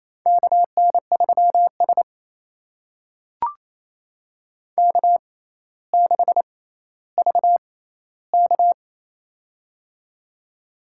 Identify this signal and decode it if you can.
Morse code, heard as a keyed tone, about 29 wpm, spaced as for 14 wpm, 700 Hz. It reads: KN3S K6VK
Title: USA Call Signs - 4 Characters - 6x Character Spacing - 28wpm